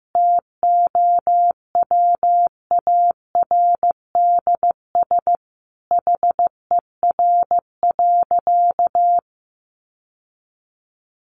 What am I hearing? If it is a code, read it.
Morse code, heard as a keyed tone, about 15 wpm, 700 Hz. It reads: TOWARDS HER.
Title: Sentences from Top 1200 Words 15wpm